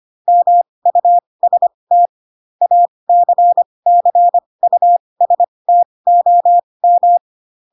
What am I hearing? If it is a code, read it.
Morse code, heard as a keyed tone, about 25 wpm, spaced as for 20 wpm, 700 Hz. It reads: MUST ACCUSTOM
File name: Pinocchio / Pinocchio ch7 15 25 ik1hge